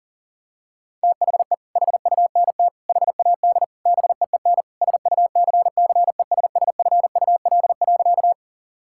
Morse code, heard as a keyed tone, about 40 wpm, 700 Hz. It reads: THE HUNT HAD BEEN SUCCESSFUL.
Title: Sentences from Top 1600 Words 40wpm